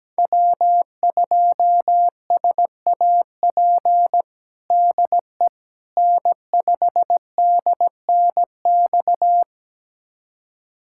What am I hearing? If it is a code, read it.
Morse code, heard as a keyed tone, about 17 wpm, 700 Hz. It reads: W2SAP DE N5DNX
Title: USA Callsign DE Callsign - 5 Characters - 17wpm